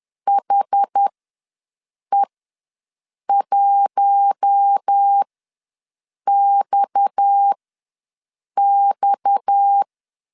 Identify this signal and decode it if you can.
Morse code, heard as a keyed tone, about 10 wpm, spaced as for 6 wpm, 800 Hz. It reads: HE1XX
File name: LEZIONI / 6 PKX0 / 6 PKX ws 10wpm 3706